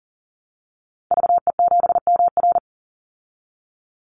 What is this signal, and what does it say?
Morse code, heard as a keyed tone, about 40 wpm, 700 Hz. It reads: VE7MR
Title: Call Signs Repeated 2x - 40wpm